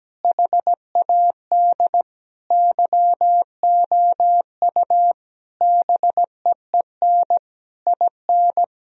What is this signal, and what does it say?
Morse code, heard as a keyed tone, about 17 wpm, 700 Hz. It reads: HAD YOU BEEN IN
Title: Sentences from Top 1000 Words - Review Part 3 - 17wpm